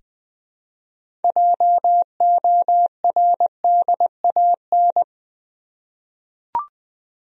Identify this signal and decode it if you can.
Morse code, heard as a keyed tone, about 21 wpm, 700 Hz. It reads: JORDAN
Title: USA Common Names - 20wpm